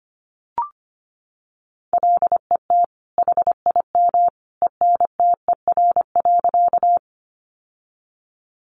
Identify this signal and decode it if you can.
Morse code, heard as a keyed tone, about 25 wpm, 700 Hz. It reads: LET HIM ENTER.
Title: Sentences from Top 1300 Words 25wpm